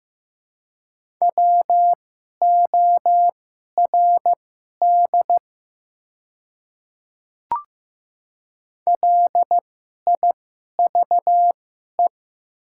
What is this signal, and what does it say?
Morse code, heard as a keyed tone, about 15 wpm, spaced as for 11 wpm, 690 Hz. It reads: WORD LIVE
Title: 2x Character Spacing - Top 500 Words - 15wpm